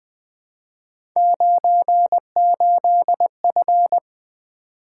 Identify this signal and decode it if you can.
Morse code, heard as a keyed tone, about 20 wpm, 700 Hz. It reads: 98F
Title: Three Letter-Number Combinations - 20wpm